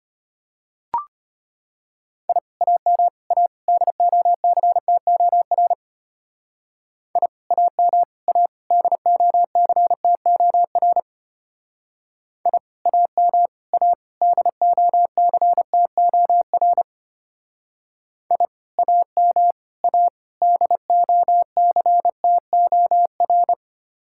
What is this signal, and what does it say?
Morse code, heard as a keyed tone, about 32 wpm, 700 Hz. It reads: I AM A DOCTOR I AM A DOCTOR I AM A DOCTOR I AM A DOCTOR